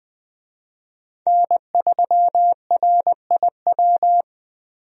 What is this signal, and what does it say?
Morse code, heard as a keyed tone, about 21 wpm, 700 Hz. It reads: N3RIW
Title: USA Call Signs - 5 Characters - Repeated 3x - 20wpm